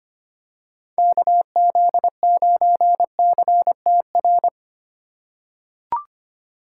Morse code, heard as a keyed tone, about 25 wpm, 700 Hz. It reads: KZ9CTR
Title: USA Call Signs - 6 Characters - 4x Character Spacing - 25wpm